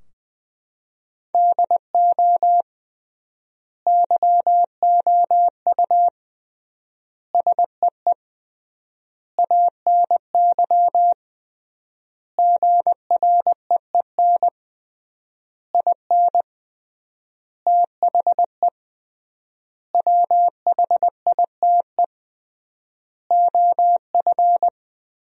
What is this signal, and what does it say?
Morse code, heard as a keyed tone, about 20 wpm, 700 Hz. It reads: DO YOU SEE ANY GREEN IN THE WHITE OF
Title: 3x Word-Spacing - Sentences from Top 500 Words - 20wpm